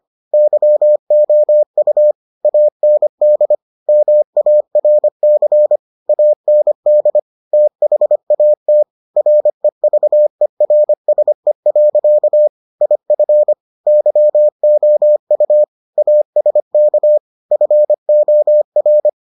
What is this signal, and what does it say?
Morse code, heard as a keyed tone, about 25 wpm, 600 Hz. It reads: YOU AND MARC AND THAT REVERSE. IF YOU ASK FOR